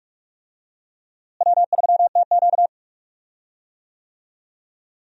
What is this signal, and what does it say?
Morse code, heard as a keyed tone, about 45 wpm, 700 Hz. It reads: W3TQ
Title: USA Call Signs - 4 Characters - Repeated 2x - 45wpm